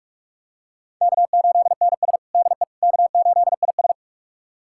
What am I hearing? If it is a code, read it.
Morse code, heard as a keyed tone, about 41 wpm, 700 Hz. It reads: K8NS DE K8IS
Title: USA Callsign DE Callsign - 4 Characters - 45wpm